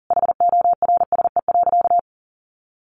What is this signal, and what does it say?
Morse code, heard as a keyed tone, about 40 wpm, 700 Hz. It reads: HORSE.
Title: Sentences from Top 1500 Words 40wpm